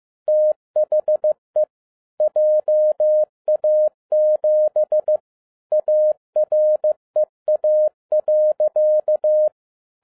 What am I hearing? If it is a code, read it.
Morse code, heard as a keyed tone, about 15 wpm, 610 Hz. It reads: THE JA7 AREA.
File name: 2011-10-mp3 / n02-FIELD DAY CONTEST QRV FROM SAKAE VILLAGE 15wpm